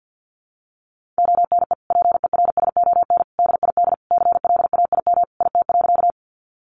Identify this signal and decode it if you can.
Morse code, heard as a keyed tone, about 50 wpm, 700 Hz. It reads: ONE PERSON DID CLAIM IT.
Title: Sentences from Top 1500 Words 50wpm